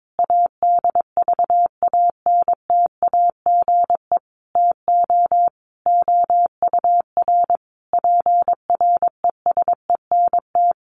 Morse code, heard as a keyed tone, about 22 wpm, 700 Hz. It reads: ADVANTAGE TO OUR PRESENT
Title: Sentences from Top 1400 Words 22wpm